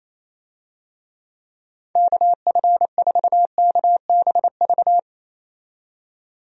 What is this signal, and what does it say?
Morse code, heard as a keyed tone, about 27 wpm, 700 Hz. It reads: KF4KBV
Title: USA Call Signs - 6 Characters - 28wpm